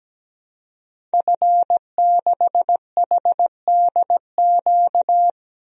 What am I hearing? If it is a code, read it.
Morse code, heard as a keyed tone, about 17 wpm, 700 Hz. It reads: F6HDQ
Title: DXCC-100 Call Signs - 5 Characters - 17wpm